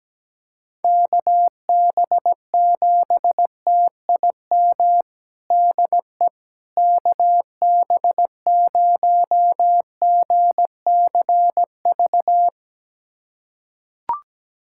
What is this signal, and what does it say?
Morse code, heard as a keyed tone, about 17 wpm, 700 Hz. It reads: KB7TIM DE KB0GCV
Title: USA Callsign DE Callsign - 6 Characters - 17wpm